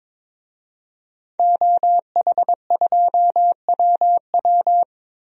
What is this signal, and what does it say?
Morse code, heard as a keyed tone, about 22 wpm, 700 Hz. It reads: OH2WW
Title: DXCC-100 Call Signs - 5 Characters - Repeated 3x - 22wpm